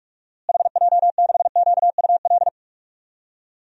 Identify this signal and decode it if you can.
Morse code, heard as a keyed tone, about 45 wpm, 700 Hz. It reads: HJ6QUL
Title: DXCC-100 Call Signs - 6 Characters - Repeated 3x - 45wpm